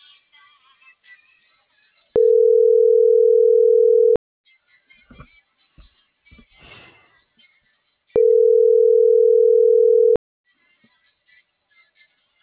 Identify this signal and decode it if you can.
Ringback tone (North America), 440 + 480 Hz.
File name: CENTRAL NMP / out patricia.martineznmp - .35349